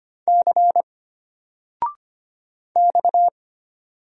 Morse code, heard as a keyed tone, about 25 wpm, 700 Hz. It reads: C X
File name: Single-Letter-Number-25wpm